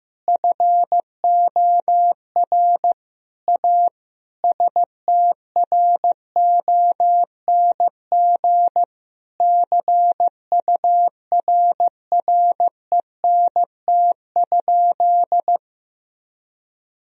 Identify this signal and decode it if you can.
Morse code, heard as a keyed tone, about 15 wpm, 700 Hz. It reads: FOR A STRONG CURRENT?
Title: Sentences from Top 1000 Words - Review Part 4 - 15wpm